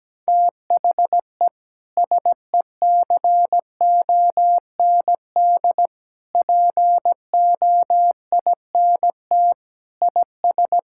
Morse code, heard as a keyed tone, about 17 wpm, 700 Hz. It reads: THE SECOND POINT IS
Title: Sentences from Top 1000 Words - Review Part 5 - 17wpm